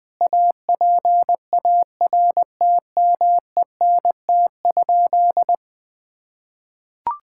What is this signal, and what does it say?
Morse code, heard as a keyed tone, about 20 wpm, 700 Hz. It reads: APARTMENT?
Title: Sentences from Top 1000 Words - Review Part 6 - 20wpm